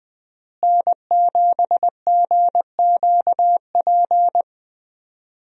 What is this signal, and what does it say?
Morse code, heard as a keyed tone, about 21 wpm, 700 Hz. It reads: N7GQP
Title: USA Call Signs - 5 Characters - Repeated 3x - 20wpm